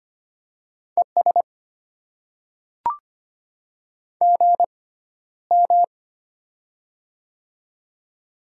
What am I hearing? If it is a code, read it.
Morse code, heard as a keyed tone, about 25 wpm, spaced as for 13 wpm, 700 Hz. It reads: ES GM
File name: Top 90 QSO Elements - 6x Character Spacing - 25wpm